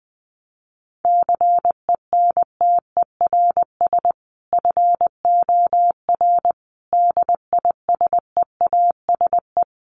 Morse code, heard as a keyed tone, about 20 wpm, 700 Hz. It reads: CENTERS FOR DISEASE
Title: Sets of 5 Words Encore2 20wpm